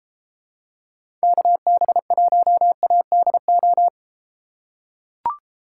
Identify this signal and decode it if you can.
Morse code, heard as a keyed tone, about 31 wpm, 700 Hz. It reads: KB1ADO